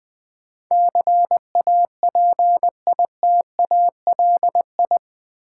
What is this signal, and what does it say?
Morse code, heard as a keyed tone, about 20 wpm, 700 Hz. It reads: CAPITALI
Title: Build A Word - 4 Letters At A Time - 10-Letter Words - 20wpm